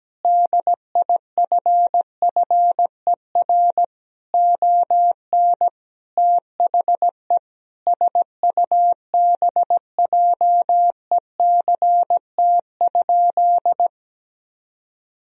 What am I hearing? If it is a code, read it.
Morse code, heard as a keyed tone, about 17 wpm, 700 Hz. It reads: DIFFER ON THE SUBJECT?
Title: Sentences from Top 1000 Words - Review Part 1 - 17wpm